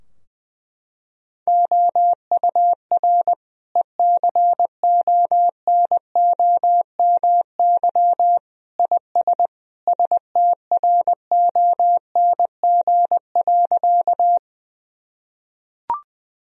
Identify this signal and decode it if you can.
Morse code, heard as a keyed tone, about 19 wpm, 700 Hz. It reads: OUR ECONOMY IS STRONG.
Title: Sentences from Top 1000 Words - Review Part 3 - 20wpm